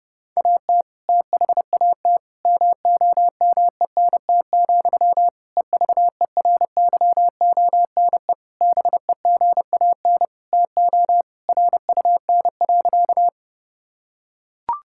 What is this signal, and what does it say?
Morse code, heard as a keyed tone, about 30 wpm, 700 Hz. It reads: AT THAT MOMENT, EVERYONE BEGAN TO RUN.
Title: Sentences from Top 1000 Words - Review Part 5 - 30wpm